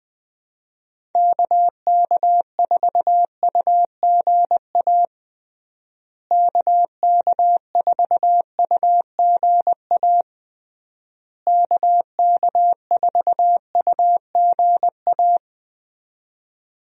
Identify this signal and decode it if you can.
Morse code, heard as a keyed tone, about 20 wpm, 700 Hz. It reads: KK4UGA KK4UGA KK4UGA
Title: USA Call Signs - 6 Characters - Repeated 3x - 20wpm